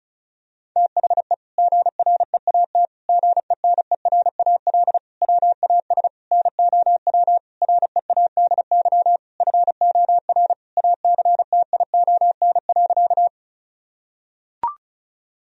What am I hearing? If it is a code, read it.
Morse code, heard as a keyed tone, about 36 wpm, 700 Hz. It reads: THE GREAT GENERAL WAS NOW READY FOR ACTION.
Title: Sentences from Top 1000 Words - Review Part 4 - 35wpm